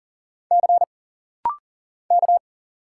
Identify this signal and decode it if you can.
Morse code, heard as a keyed tone, about 40 wpm, 700 Hz. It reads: C K
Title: Single Letter-Number - Rapid-Fire 40wpm